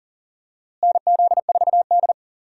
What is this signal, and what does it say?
Morse code, heard as a keyed tone, about 40 wpm, 700 Hz. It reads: NZ4D
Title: USA Call Signs - 4 Characters - Repeated 2x - 40wpm